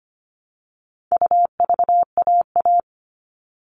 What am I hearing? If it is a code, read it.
Morse code, heard as a keyed tone, about 25 wpm, 700 Hz. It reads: UVAA
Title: All Cut Numbers - 4-Digit 25wpm